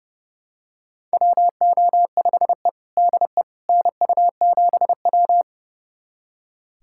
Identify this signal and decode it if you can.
Morse code, heard as a keyed tone, about 32 wpm, 700 Hz. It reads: WO5E DE NU7W
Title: USA Callsign DE Callsign - 4 Characters - 30wpm